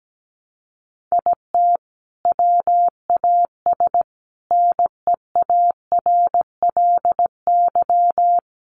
Morse code, heard as a keyed tone, about 17 wpm, 700 Hz. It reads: IT WAS NEARLY
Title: Sentences from Top 1300 Words 17wpm